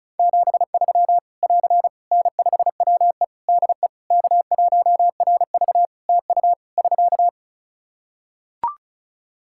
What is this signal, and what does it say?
Morse code, heard as a keyed tone, about 35 wpm, 700 Hz. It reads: 73 <AR> N5WE DE K1RV TU <SK>
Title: Practice QSO - Exchange 4 - Ending - 35wpm